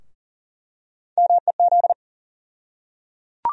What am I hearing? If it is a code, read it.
Morse code, heard as a keyed tone, about 40 wpm, 700 Hz. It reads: MEZ
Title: USA Call Sign Postfixes - 3 Letters 40wpm